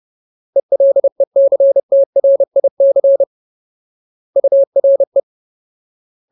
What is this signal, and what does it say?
Morse code, heard as a keyed tone, about 30 wpm, 550 Hz. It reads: ELECTRIC URE
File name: Ultimate Word List / 30WPM Ultimate